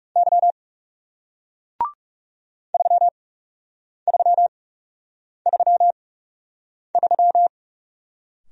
Morse code, heard as a keyed tone, about 39 wpm, spaced as for 8 wpm, 700 Hz. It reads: Y 3333